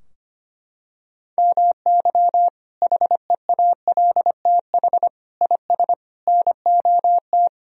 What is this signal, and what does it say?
Morse code, heard as a keyed tone, about 25 wpm, 700 Hz. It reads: MY HEALTH IS NOT
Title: Sentences from Top 1000 Words - Review Part 3 - 25wpm